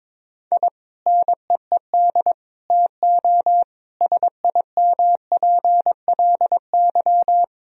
Morse code, heard as a keyed tone, about 22 wpm, 700 Hz. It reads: I NEED TO SIMPLY
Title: Sentences from Top 1000 Words - Review Part 4 - 22wpm